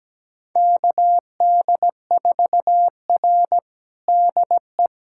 Morse code, heard as a keyed tone, about 17 wpm, 700 Hz. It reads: KD4R DE
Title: USA Callsign DE Callsign - 4 Characters - 17wpm